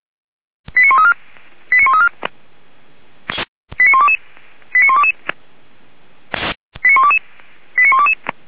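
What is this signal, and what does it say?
Five-tone selective call (ZVEI), heard as quick runs of five tones, tones of about 70 ms. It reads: 89134 89134 89122 89122 89122 89122
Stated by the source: Received by SDR, Feuerwehr Orlamünde.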